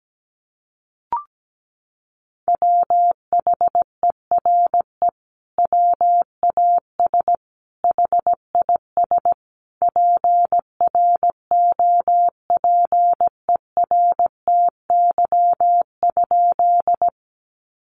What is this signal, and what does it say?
Morse code, heard as a keyed tone, about 17 wpm, 700 Hz. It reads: WHERE WAS HIS PROPERTY?
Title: Sentences from Top 1000 Words 17wpm